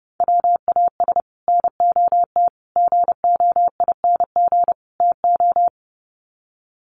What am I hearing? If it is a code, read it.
Morse code, heard as a keyed tone, about 30 wpm, 700 Hz. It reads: WAS NOT GOING TO
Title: Sets of 4 Words 30wpm